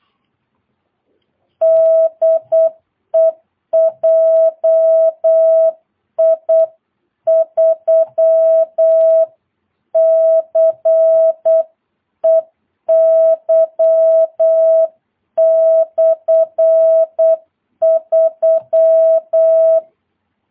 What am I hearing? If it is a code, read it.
Morse code, heard as a keyed tone, about 8 wpm, 660 Hz. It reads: DE JI3CEY/3
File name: jh8cbh / 2022cw